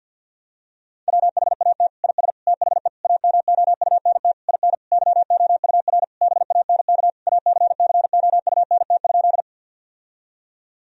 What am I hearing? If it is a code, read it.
Morse code, heard as a keyed tone, about 50 wpm, 700 Hz. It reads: WHAT IS THE AMOUNT IN YOUR BANK ACCOUNT?
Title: Sentences from Top 1000 Words - Review Part 4 - 50wpm